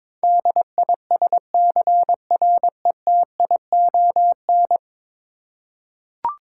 Morse code, heard as a keyed tone, about 23 wpm, 700 Hz. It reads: DISCRETION
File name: practice / 22wpm / 10 Letter Words - First 3 Letters Given - 22wpm